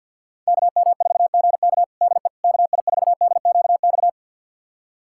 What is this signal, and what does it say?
Morse code, heard as a keyed tone, about 48 wpm, 700 Hz. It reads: KM4GK DE KI4DQX
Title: USA Callsign DE Callsign 50wpm